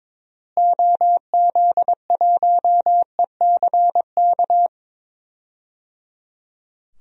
Morse code, heard as a keyed tone, about 22 wpm, 700 Hz. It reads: OZ1ECK
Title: DXCC-100 Call Signs - 6 Characters - Repeated 3x - 22wpm